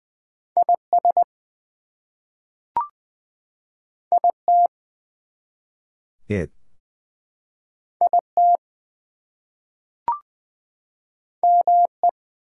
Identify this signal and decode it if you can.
Morse code, heard as a keyed tone, about 20 wpm, 700 Hz. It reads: IS IT IT ME
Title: Two Letter Words - 20wpm